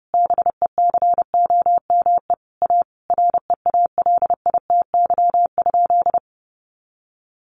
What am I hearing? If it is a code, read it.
Morse code, heard as a keyed tone, about 30 wpm, 700 Hz. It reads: BECOME A REALITY?
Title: Sentences from Top 1200 Words 30wpm